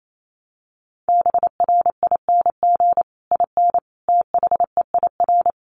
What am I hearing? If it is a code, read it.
Morse code, heard as a keyed tone, about 28 wpm, 700 Hz. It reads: BRING IN THEIR
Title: Sentences from Top 1300 Words 28wpm